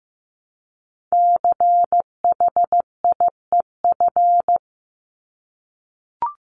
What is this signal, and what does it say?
Morse code, heard as a keyed tone, about 15 wpm, 700 Hz. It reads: CHIEF
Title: LY Suffix - Top 100 Words with Base Words 15wpm